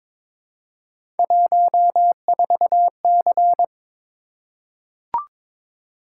Morse code, heard as a keyed tone, about 22 wpm, 700 Hz. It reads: 14C
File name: practice / 22wpm / Three Letter-Number Combinations - 22wpm